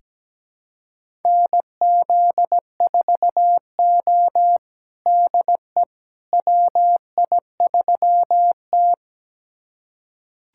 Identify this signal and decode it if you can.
Morse code, heard as a keyed tone, about 17 wpm, 700 Hz. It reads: NZ4O DE WI3T
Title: USA Callsign DE Callsign - 4 Characters - 17wpm